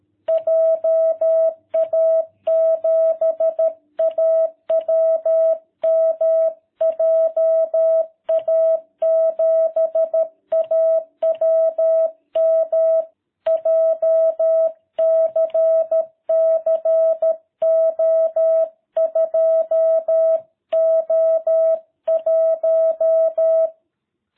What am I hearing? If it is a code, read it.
Morse code, heard as a keyed tone, about 13 wpm, 660 Hz. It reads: JA7AWM JA7AWM JCC O2O1